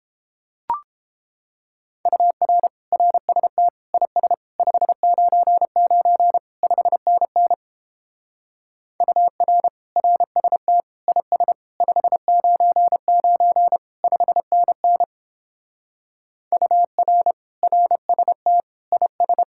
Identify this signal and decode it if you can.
Morse code, heard as a keyed tone, about 29 wpm, 700 Hz. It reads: UR RST IS 599 5NN UR RST IS 599 5NN UR RST IS